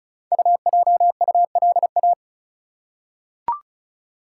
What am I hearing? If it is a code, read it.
Morse code, heard as a keyed tone, about 36 wpm, 700 Hz. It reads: UJULA
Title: Code Groups - 5 Letters - 35wpm